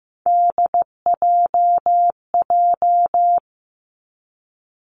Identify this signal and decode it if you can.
Morse code, heard as a keyed tone, about 16 wpm, 700 Hz. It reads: DJJ
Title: Three Challenging Letter-Number Combinations 15wpm